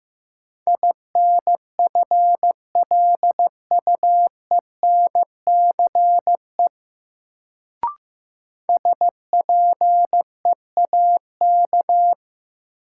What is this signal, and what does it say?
Morse code, heard as a keyed tone, about 15 wpm, 700 Hz. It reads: INFLUENCE SPEAK
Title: Sentences from Top 1100 Words - New Word Drill Alt - 15wpm